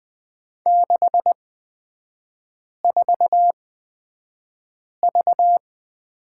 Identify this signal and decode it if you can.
Morse code, heard as a keyed tone, about 20 wpm, spaced as for 5 wpm, 700 Hz. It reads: 64V
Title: 236 Lesson 38e - Letter q - Three Characters 20wpm